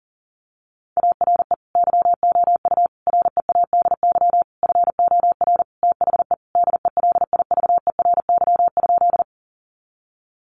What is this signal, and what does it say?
Morse code, heard as a keyed tone, about 37 wpm, 700 Hz. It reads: ARE YOU READY FOR THE DELIVERY?
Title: Sentences from Top 1500 Words 40wpm